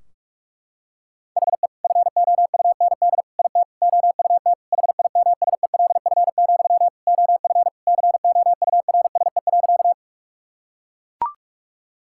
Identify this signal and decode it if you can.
Morse code, heard as a keyed tone, about 41 wpm, 700 Hz. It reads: HE FOUND IT OUT HIMSELF, OF COURSE.
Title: Sentences from Top 1600 Words 45wpm